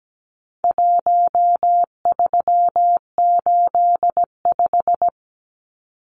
Morse code, heard as a keyed tone, about 17 wpm, 700 Hz. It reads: 1385
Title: Numbers - 4-Digit 17wpm